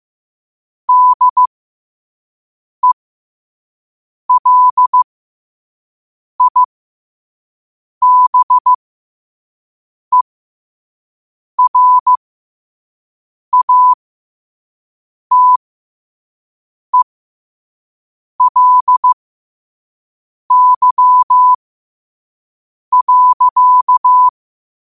Morse code, heard as a keyed tone, about 15 wpm, spaced as for 5 wpm, 1000 Hz. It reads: DELIBERATELY.
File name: qotdfiles / 5wpm